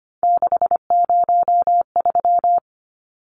Morse code, heard as a keyed tone, about 25 wpm, 700 Hz. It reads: 603
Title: Numbers - 3-Digit 25wpm